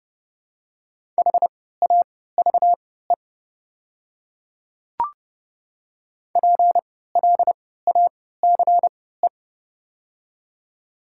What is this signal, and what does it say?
Morse code, heard as a keyed tone, about 30 wpm, spaced as for 17 wpm, 700 Hz. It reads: HAVE PLACE
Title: 3x Character Spacing - Top 200 Words - 30wpm